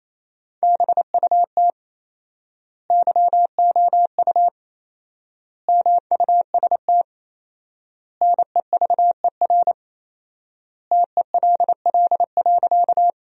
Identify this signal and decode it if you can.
Morse code, heard as a keyed tone, about 28 wpm, 700 Hz. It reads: BUT YOU MUST NEVER TELL.
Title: 4x Word-Spacing - Sentences from Top 300 Words - 28wpm